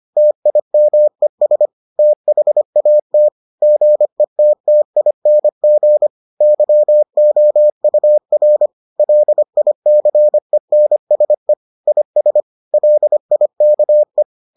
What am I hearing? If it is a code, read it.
Morse code, heard as a keyed tone, about 25 wpm, 600 Hz. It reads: TIMES THAT GETTING YOUR LICENSE IS LIKE